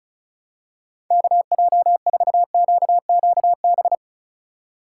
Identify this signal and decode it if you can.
Morse code, heard as a keyed tone, about 34 wpm, 700 Hz. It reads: KJ4QQB
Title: USA Call Signs - 6 Characters - 2x Character Spacing - 35wpm